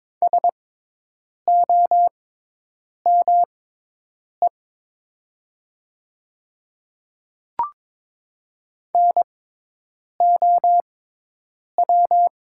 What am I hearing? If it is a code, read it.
Morse code, heard as a keyed tone, about 22 wpm, spaced as for 8 wpm, 700 Hz. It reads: SOME NOW
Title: 6x Character Spacing - Top 100 Words - 22wpm